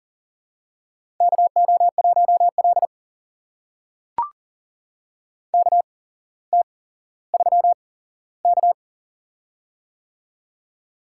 Morse code, heard as a keyed tone, about 42 wpm, spaced as for 17 wpm, 700 Hz. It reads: KO1L KT3K
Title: USA Call Signs - 4 Characters - 8x Character Spacing - 40wpm